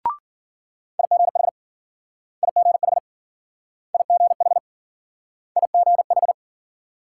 Morse code, heard as a keyed tone, about 53 wpm, 700 Hz. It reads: IGH IGH IGH IGH